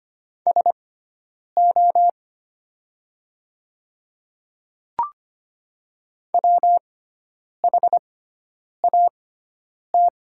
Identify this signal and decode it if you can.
Morse code, heard as a keyed tone, about 25 wpm, spaced as for 9 wpm, 700 Hz. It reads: SO WHAT
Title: 6x Character Spacing - Top 200 Words - 25wpm